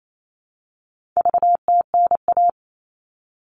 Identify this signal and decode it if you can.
Morse code, heard as a keyed tone, about 28 wpm, 700 Hz. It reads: VTNA
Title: All Cut Numbers - 4-Digit 28wpm